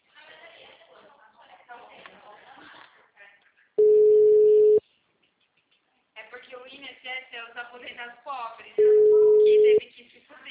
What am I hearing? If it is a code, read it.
Ringback tone (Europe, 425 Hz), with speech.
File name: out- -2002- - - .2071